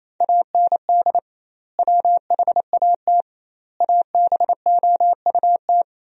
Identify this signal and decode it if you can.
Morse code, heard as a keyed tone, about 28 wpm, 700 Hz. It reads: AND WHAT ABOUT
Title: 2x Word-Spacing - Sentences from Top 200 Words - 28wpm